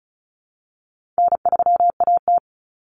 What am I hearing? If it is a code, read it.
Morse code, heard as a keyed tone, about 34 wpm, 700 Hz. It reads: N3AT
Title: Common Cut Numbers - 4-Digit 35wpm